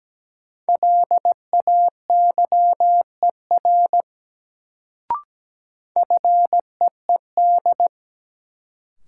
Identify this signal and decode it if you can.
Morse code, heard as a keyed tone, about 17 wpm, 700 Hz. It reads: LAYER FEED
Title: Sentences from Top 1600 Words - New Word Drill - 17wpm